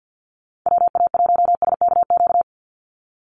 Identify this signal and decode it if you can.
Morse code, heard as a keyed tone, about 50 wpm, 700 Hz. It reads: WA1SKQ
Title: Call Signs Repeated 2x - 50wpm